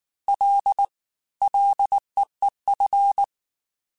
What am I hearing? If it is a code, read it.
Morse code, heard as a keyed tone, about 19 wpm, 800 Hz. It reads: L LEEF
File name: Ls / Single L m3s / Single L @18